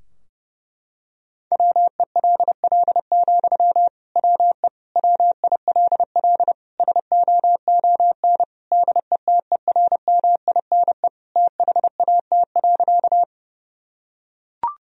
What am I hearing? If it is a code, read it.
Morse code, heard as a keyed tone, about 30 wpm, 700 Hz. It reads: WELL, WE WILL SOON DETERMINE THAT.
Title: Sentences from Top 1000 Words - Review Part 4 - 30wpm